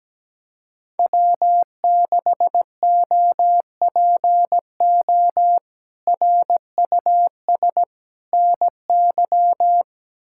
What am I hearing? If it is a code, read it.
Morse code, heard as a keyed tone, about 17 wpm, 700 Hz. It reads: W6OPO RUS NY
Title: K1USN SST QSO Practice - 17wpm